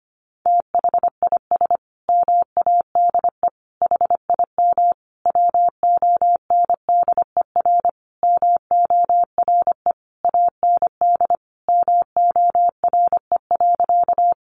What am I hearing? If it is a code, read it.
Morse code, heard as a keyed tone, about 25 wpm, 700 Hz. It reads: THIS MADE HIM WONDER MORE AND MORE.
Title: Sentences from Top 500 Words 25wpm